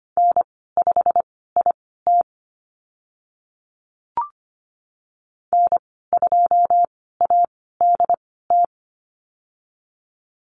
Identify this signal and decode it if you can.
Morse code, heard as a keyed tone, about 25 wpm, spaced as for 16 wpm, 700 Hz. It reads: N5IT N2ADT
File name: Lesson 15g - Number 5 - Call Signs 25wpm